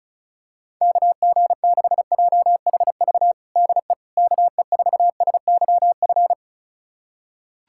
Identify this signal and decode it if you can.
Morse code, heard as a keyed tone, about 36 wpm, 700 Hz. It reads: KG6JHV DE KE4SYF
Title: USA Callsign DE Callsign - 6 Characters - 35wpm